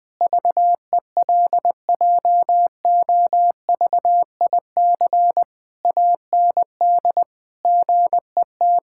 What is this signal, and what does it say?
Morse code, heard as a keyed tone, about 21 wpm, 700 Hz. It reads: VELJOVIC AND GET
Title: Pangram Challenge v2 - 20wpm